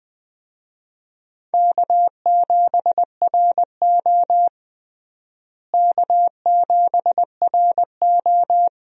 Morse code, heard as a keyed tone, about 20 wpm, 700 Hz. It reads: K7RO K7RO
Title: USA Call Signs - 4 Characters - Repeated 2x - 20wpm